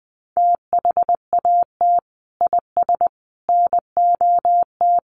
Morse code, heard as a keyed tone, about 20 wpm, 700 Hz. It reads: THAT IS NOT
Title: Sentences from Top 600 Words 20wpm